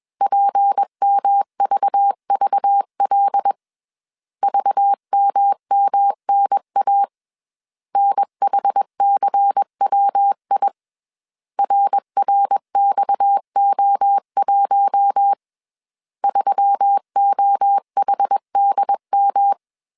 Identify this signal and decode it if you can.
Morse code, heard as a keyed tone, about 21 wpm, 800 Hz. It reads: PMVVL VMMNA NHCWI RRXO1 3OHDM
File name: CW / DET / 1 6 1500C S A 20wpm 2048